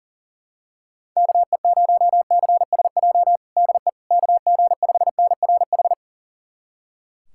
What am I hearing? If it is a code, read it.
Morse code, heard as a keyed tone, about 40 wpm, 700 Hz. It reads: KE0CSJ DE KG5NRH
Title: USA Callsign DE Callsign - 6 Characters - 40wpm